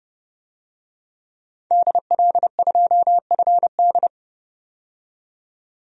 Morse code, heard as a keyed tone, about 32 wpm, 700 Hz. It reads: DL2FD
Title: DXCC-100 Call Signs - 5 Characters - 30wpm